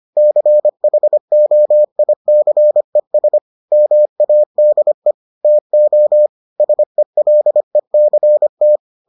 Morse code, heard as a keyed tone, about 25 wpm, 600 Hz. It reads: CHOICES MADE TO SELECT